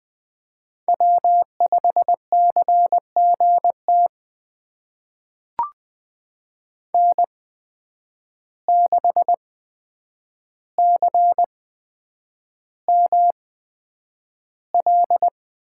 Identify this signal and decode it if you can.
Morse code, heard as a keyed tone, about 20 wpm, spaced as for 9 wpm, 700 Hz. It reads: W5CGT N6CML
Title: USA Call Signs - 5 Characters - 8x Character Spacing - 20wpm